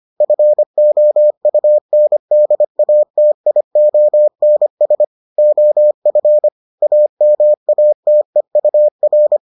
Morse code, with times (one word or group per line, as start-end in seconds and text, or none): 0.2-5.0 FOUNDATIONS
5.4-6.5 OF
6.8-9.4 AMATEUR